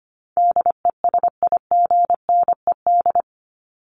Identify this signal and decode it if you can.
Morse code, heard as a keyed tone, about 25 wpm, 700 Hz. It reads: DESIGNED
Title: ED Suffix - Top 100 Words with Base Words 25wpm